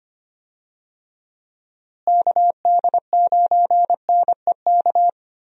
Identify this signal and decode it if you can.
Morse code, heard as a keyed tone, about 25 wpm, 700 Hz. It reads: KD9NEK
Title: USA Call Signs - 6 Characters - Repeated 3x - 25wpm